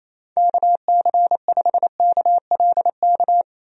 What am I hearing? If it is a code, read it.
Morse code, heard as a keyed tone, about 28 wpm, 700 Hz. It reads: KC5KLK
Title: USA Call Signs - 6 Characters - Repeated 2x - 28wpm